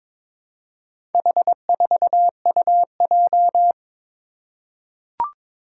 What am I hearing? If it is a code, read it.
Morse code, heard as a keyed tone, about 22 wpm, 700 Hz. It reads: H4UJ
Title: Four Challenging Letter-Number Combinations 22wpm